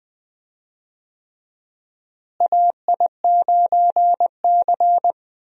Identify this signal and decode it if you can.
Morse code, heard as a keyed tone, about 20 wpm, 700 Hz. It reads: AI9C
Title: USA Call Signs - 4 Characters - Repeated 3x - 20wpm